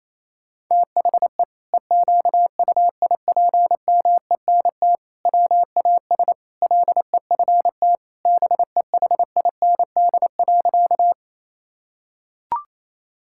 Morse code, heard as a keyed tone, about 28 wpm, 700 Hz. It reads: THE EQUIPMENT WAS LEFT BEHIND.
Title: Sentences from Top 1000 Words - Review Part 5 - 28wpm